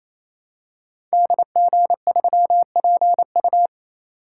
Morse code, heard as a keyed tone, about 27 wpm, 700 Hz. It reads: DG3PU
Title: Code Groups - 5 Letter:Numbers - 28wpm